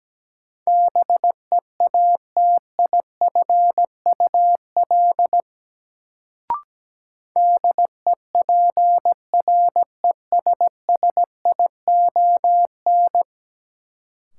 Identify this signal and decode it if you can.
Morse code, heard as a keyed tone, about 17 wpm, 700 Hz. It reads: BEATIFUL DEPRESSION
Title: Sentences from Top 1300 Words - New Word Drill Alt - 17wpm